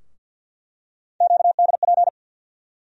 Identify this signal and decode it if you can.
Morse code, heard as a keyed tone, about 47 wpm, 700 Hz. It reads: QDP
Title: USA Call Sign Postfixes - 3 Letters 50wpm